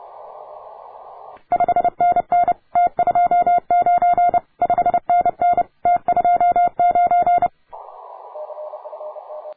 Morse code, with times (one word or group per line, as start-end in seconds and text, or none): 1.5-2.5 5NN
2.8-4.4 T29
4.6-5.6 5NN
5.9-7.5 T29